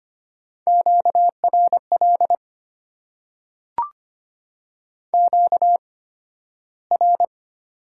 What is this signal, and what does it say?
Morse code, heard as a keyed tone, about 25 wpm, spaced as for 13 wpm, 700 Hz. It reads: QRL QR
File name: Top 90 QSO Elements - 8x Character Spacing - 25wpm